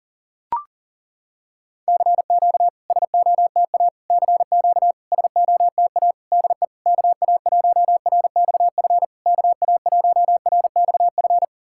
Morse code, heard as a keyed tone, about 40 wpm, 690 Hz. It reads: CQ SOTA CQ SOTA DE KA1RXF KA1RXF
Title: SOTA QSO Practice 40wpm